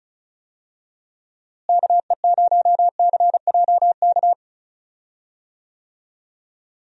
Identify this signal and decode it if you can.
Morse code, heard as a keyed tone, about 36 wpm, 700 Hz. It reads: KE0CJK
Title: USA Call Signs - 6 Characters - Repeated 2x - 35wpm